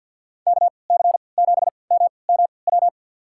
Morse code, heard as a keyed tone, about 50 wpm, spaced as for 28 wpm, 700 Hz. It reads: KC7MMW
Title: USA Call Signs - 6 Characters - 3x Character Spacing - 50wpm